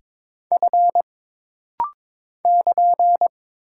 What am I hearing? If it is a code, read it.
Morse code, heard as a keyed tone, about 22 wpm, 700 Hz. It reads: F <KN>
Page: This rapid-fire timing is geared to push your ICR (Instant Character Recognition) skills.